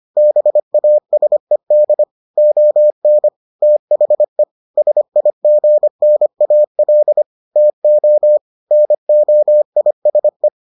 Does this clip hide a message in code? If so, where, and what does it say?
Yes: Morse code from 0.2 to 10.5 s, heard as a keyed tone, about 25 wpm, 600 Hz. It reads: BASED ON THE SIGNAL TO NOISE